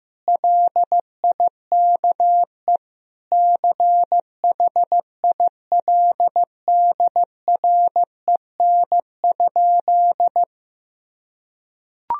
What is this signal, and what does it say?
Morse code, heard as a keyed tone, about 16 wpm, 700 Hz. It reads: LIKE CHILDREN?
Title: Sentences from Top 1000 Words - Review Part 2 - 15wpm